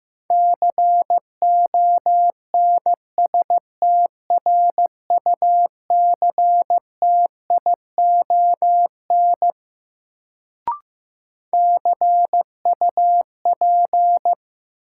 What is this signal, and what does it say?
Morse code, heard as a keyed tone, about 15 wpm, 700 Hz. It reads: CONSTRUCTION CUP
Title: Sentences from Top 1600 Words - New Word Drill Alt - 15wpm